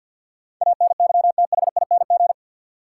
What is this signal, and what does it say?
Morse code, heard as a keyed tone, about 52 wpm, 700 Hz. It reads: ANYTHING
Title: Sentences from Top 1000 Words - New Word Drill - 50wpm